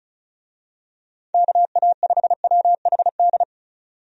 Morse code, heard as a keyed tone, about 34 wpm, 700 Hz. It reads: KA5WHD
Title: USA Call Signs - 6 Characters - 35wpm